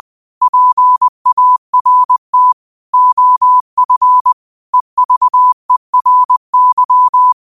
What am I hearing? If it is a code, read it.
Morse code, heard as a keyed tone, about 20 wpm, 1000 Hz. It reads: PART OF EVERY
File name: qotdfiles / 20wpm